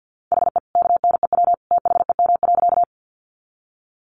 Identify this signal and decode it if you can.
Morse code, heard as a keyed tone, about 50 wpm, 700 Hz. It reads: HE KNEW THEM.
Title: Sentences from Top 400 Words 50wpm